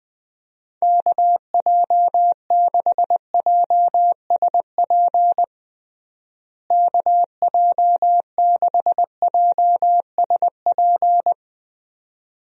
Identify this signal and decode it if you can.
Morse code, heard as a keyed tone, about 21 wpm, 700 Hz. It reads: KJ6JSP KJ6JSP
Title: USA Call Signs - 6 Characters - Repeated 2x - 20wpm